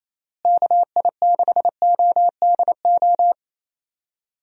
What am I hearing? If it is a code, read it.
Morse code, heard as a keyed tone, about 28 wpm, 700 Hz. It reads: KI6ODO
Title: USA Call Signs - 6 Characters - Repeated 3x - 28wpm